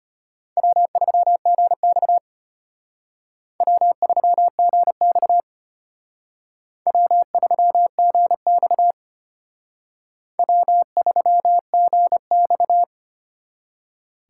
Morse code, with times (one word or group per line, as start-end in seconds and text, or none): 0.6-2.2 W3GX
3.6-5.4 W3GX
6.9-8.9 W3GX
10.4-12.8 W3GX